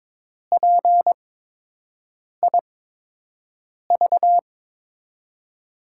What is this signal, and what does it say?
Morse code, heard as a keyed tone, about 22 wpm, spaced as for 6 wpm, 700 Hz. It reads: PIV
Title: USA Call Signs - 6 Characters - 8x Character Spacing - 22wpm